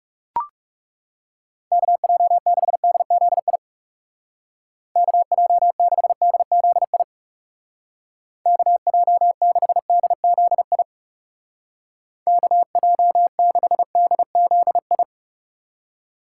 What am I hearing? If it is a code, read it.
Morse code, heard as a keyed tone, about 37 wpm, 700 Hz. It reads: KJ6DZI KJ6DZI KJ6DZI KJ6DZI
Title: Speed Racing - USA Call Signs 30wpm